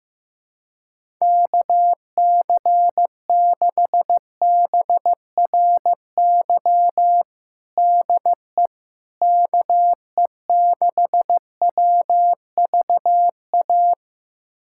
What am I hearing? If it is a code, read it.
Morse code, heard as a keyed tone, about 15 wpm, 700 Hz. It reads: KC6BRY DE KE6WVA
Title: USA Callsign DE Callsign - 6 Characters - 15wpm